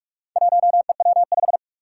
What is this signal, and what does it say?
Morse code, heard as a keyed tone, about 44 wpm, 700 Hz. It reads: 1EW5
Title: Four Challenging Letter-Number Combinations 45wpm